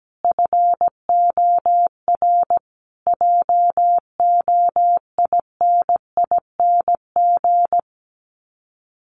Morse code, heard as a keyed tone, about 17 wpm, 700 Hz. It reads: FOR JOINING
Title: Sets of 5 Words Encore 17wpm